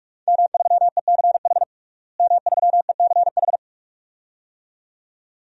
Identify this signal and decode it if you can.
Morse code, heard as a keyed tone, about 41 wpm, 700 Hz. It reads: M3ECH M3ECH
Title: DXCC-100 Call Signs - 5 Characters - Repeated 2x - 45wpm